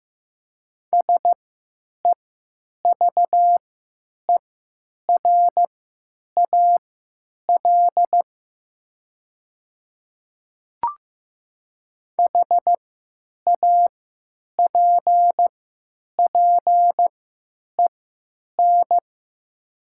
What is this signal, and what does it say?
Morse code, heard as a keyed tone, about 15 wpm, spaced as for 9 wpm, 700 Hz. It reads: SEVERAL HAPPEN
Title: 3x Character Spacing - Top 500 Words - 15wpm